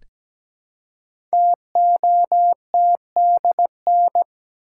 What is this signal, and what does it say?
Morse code, heard as a keyed tone, about 17 wpm, 700 Hz. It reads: TOTDN
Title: Code Groups - 5 Letters - 17wpm